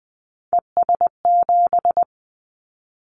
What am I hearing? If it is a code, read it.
Morse code, heard as a keyed tone, about 20 wpm, 700 Hz. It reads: ES7
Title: Callsign Prefixes for Top 50 DXCC - 20wpm